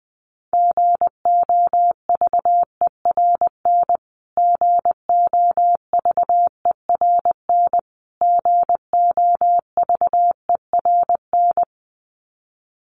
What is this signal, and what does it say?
Morse code, heard as a keyed tone, about 19 wpm, 700 Hz. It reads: GOVERN GOVERN GOVERN
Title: Top 500 Words Repeated 20wpm